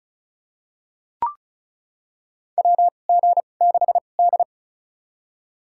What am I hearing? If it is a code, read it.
Morse code, heard as a keyed tone, about 34 wpm, spaced as for 25 wpm, 700 Hz. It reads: WG6D
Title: USA Call Signs - 2x Character Spacing - 35wpm